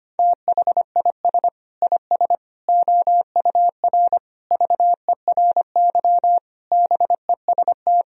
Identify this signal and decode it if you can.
Morse code, heard as a keyed tone, about 25 wpm, 700 Hz. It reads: THIS IS OUR VERY BEST
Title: Sentences from Top 1000 Words - Review Part 4 - 25wpm